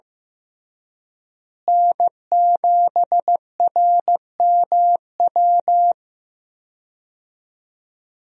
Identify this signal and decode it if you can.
Morse code, heard as a keyed tone, about 15 wpm, 700 Hz. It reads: N7RMW
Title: USA Call Signs - 5 Characters - Repeated 3x - 15wpm